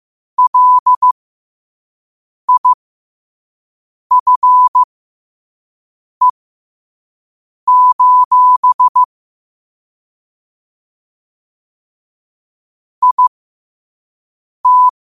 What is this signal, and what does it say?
Morse code, heard as a keyed tone, about 15 wpm, spaced as for 5 wpm, 1000 Hz. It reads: LIFE: IT
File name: 5wpm